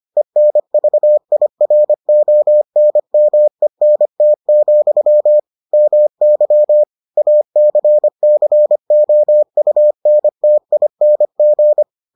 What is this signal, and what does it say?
Morse code, heard as a keyed tone, about 25 wpm, 600 Hz. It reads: ENVIRONMENT, MY ACCOUNTING